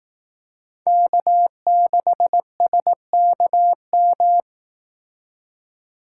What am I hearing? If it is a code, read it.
Morse code, heard as a keyed tone, about 18 wpm, 700 Hz. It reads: K6SKM
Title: Speed Racing - USA 1x3 Call Signs 15wpm